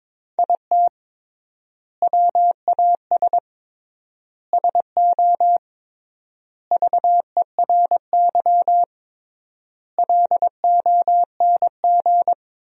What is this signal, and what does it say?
Morse code, heard as a keyed tone, about 22 wpm, 700 Hz. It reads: IT WAS SO VERY LONG
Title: 3x Word-Spacing - Sentences from Top 500 Words - 22wpm